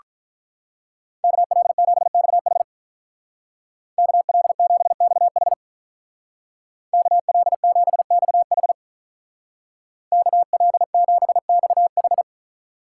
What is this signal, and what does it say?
Morse code, heard as a keyed tone, about 44 wpm, 700 Hz. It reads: KL7XH KL7XH KL7XH KL7XH